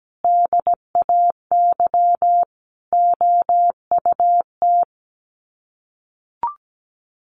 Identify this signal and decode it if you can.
Morse code, heard as a keyed tone, about 17 wpm, 700 Hz. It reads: DAY OUT
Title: Sets of 5 Words Encore2 17wpm